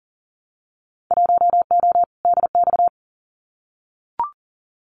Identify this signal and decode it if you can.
Morse code, heard as a keyed tone, about 43 wpm, 700 Hz. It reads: 1O DX
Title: Winter Field Day Exchanges Encore 40wpm